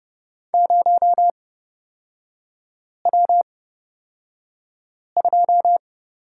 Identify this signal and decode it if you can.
Morse code, heard as a keyed tone, about 30 wpm, spaced as for 5 wpm, 700 Hz. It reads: 0W2
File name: Lesson 37e - Letter x - Three Characters 30wpm